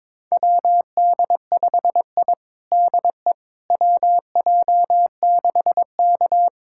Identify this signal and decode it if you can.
Morse code, heard as a keyed tone, about 22 wpm, 700 Hz. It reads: WD5I DE WJ6K
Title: USA Callsign DE Callsign - 4 Characters - 22wpm